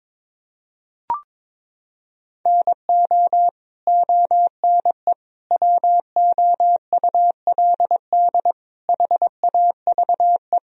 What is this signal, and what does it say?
Morse code, heard as a keyed tone, about 22 wpm, 700 Hz. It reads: NO ONE WOULD HAVE
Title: Sentences from Top 1000 Words - Review Part 1 - 22wpm